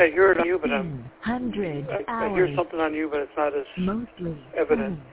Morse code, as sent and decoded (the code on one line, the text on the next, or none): none